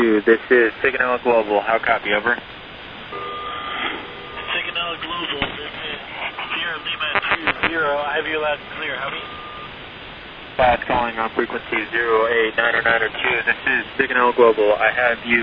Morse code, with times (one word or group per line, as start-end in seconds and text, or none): none